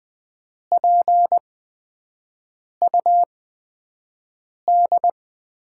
Morse code, heard as a keyed tone, about 20 wpm, spaced as for 5 wpm, 700 Hz. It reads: PUD
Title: USA Call Signs - 8x Character Spacing - 20wpm